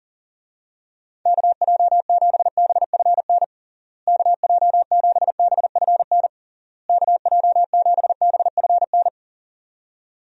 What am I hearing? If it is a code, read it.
Morse code, heard as a keyed tone, about 40 wpm, 700 Hz. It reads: KJ7BFN KJ7BFN KJ7BFN
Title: USA Call Signs - 6 Characters - Repeated 3x - 40wpm